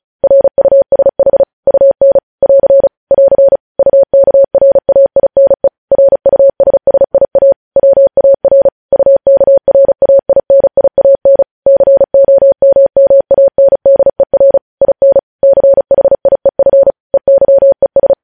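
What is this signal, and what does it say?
Morse code, heard as a keyed tone, about 36 wpm, 550 Hz. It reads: RUSH UN <AR> <AR> UKRAINE RUSSIA WAR UKRAINIAN COMMANDER IN CHIEF EYES
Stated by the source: Morse code news headlines sent at 35.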